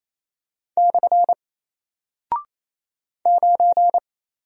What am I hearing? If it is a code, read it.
Morse code, heard as a keyed tone, about 28 wpm, 700 Hz. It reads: / 9
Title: Single Character (Letter:Number:Punctuation:Prosign) Alt 28wpm